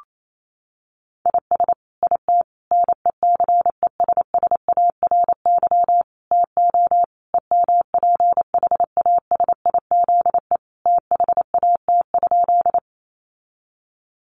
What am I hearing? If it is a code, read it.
Morse code, heard as a keyed tone, about 28 wpm, 700 Hz. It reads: IS IT NECESSARY TO EMPHASIZE THAT?
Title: Sentences from Top 1500 Words 28wpm